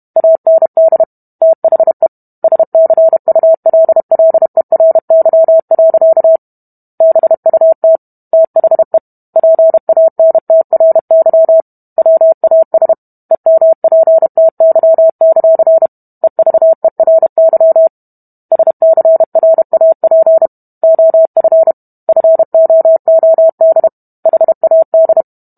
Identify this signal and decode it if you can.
Morse code, heard as a keyed tone, about 32 wpm, 650 Hz. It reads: AND THE SCULLERY. BUT THE PANTRY WAS EMPTY; EVERY SCRAP OF FOOD HAD